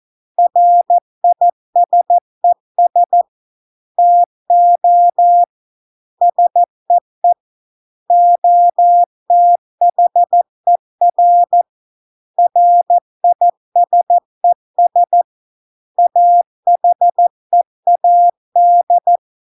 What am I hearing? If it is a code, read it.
Morse code, heard as a keyed tone, about 14 wpm, 700 Hz. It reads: RISES TO SEE OTHER RISES AHEAD